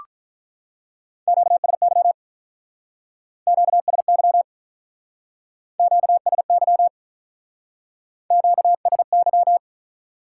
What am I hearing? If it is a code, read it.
Morse code, heard as a keyed tone, about 46 wpm, 700 Hz. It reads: QSY QSY QSY QSY